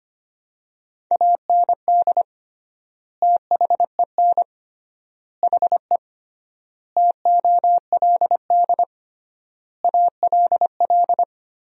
Morse code, heard as a keyed tone, about 25 wpm, 700 Hz. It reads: AND THEN HE TOLD ALL
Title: 3x Word-Spacing - Sentences from Top 400 Words - 25wpm